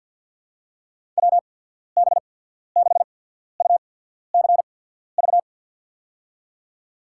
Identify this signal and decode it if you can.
Morse code, heard as a keyed tone, about 53 wpm, spaced as for 14 wpm, 690 Hz. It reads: WB6UCV
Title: USA Call Signs - 6 Characters - 8x Character Spacing - 50wpm